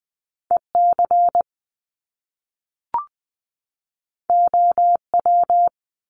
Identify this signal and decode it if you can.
Morse code, heard as a keyed tone, about 20 wpm, 700 Hz. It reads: EC OW
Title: Top Two Letter Combinations in Words 20wpm